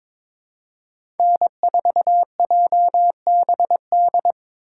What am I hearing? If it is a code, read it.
Morse code, heard as a keyed tone, about 22 wpm, 700 Hz. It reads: N4JBD
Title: USA Call Signs - 5 Characters - 3x Character Spacing - 22wpm